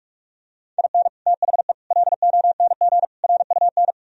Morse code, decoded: IN THE LONG RUN

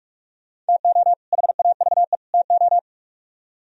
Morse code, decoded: TO HAVE TO

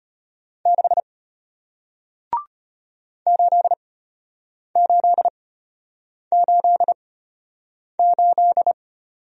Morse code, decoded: 6 8888

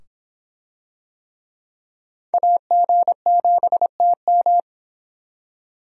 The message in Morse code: AG7TM

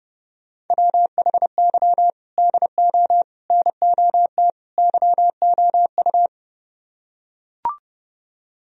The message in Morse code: WHY DO NOT YOU

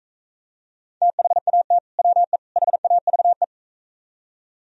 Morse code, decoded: THAT WE HAVE

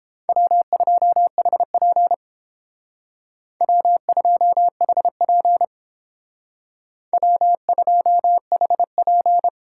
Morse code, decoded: W2HP W2HP W2HP

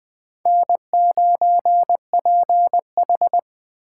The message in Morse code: N9PH

